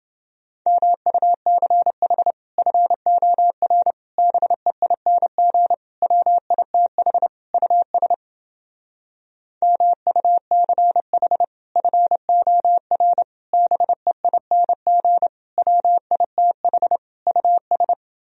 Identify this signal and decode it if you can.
Morse code, heard as a keyed tone, about 30 wpm, 700 Hz. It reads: MUCH FOR BEING WITH US MUCH FOR BEING WITH US